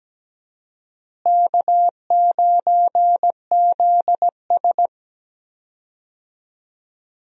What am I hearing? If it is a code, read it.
Morse code, heard as a keyed tone, about 17 wpm, 700 Hz. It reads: K9ZS
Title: USA Call Signs - 4 Characters - Repeated 3x - 17wpm